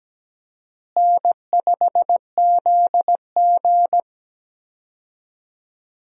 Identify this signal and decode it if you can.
Morse code, heard as a keyed tone, about 17 wpm, 700 Hz. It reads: N5ZG
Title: USA Call Signs - 4 Characters - Repeated 2x - 17wpm